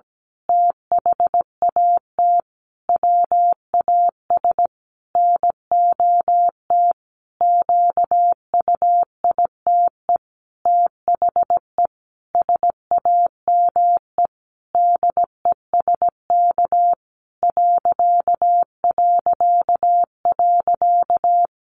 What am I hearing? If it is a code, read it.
Morse code, heard as a keyed tone, about 17 wpm, 700 Hz. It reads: THAT WAS NOT QUITE THE SAME DESK ...
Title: A Sound Of Thunder 17wpm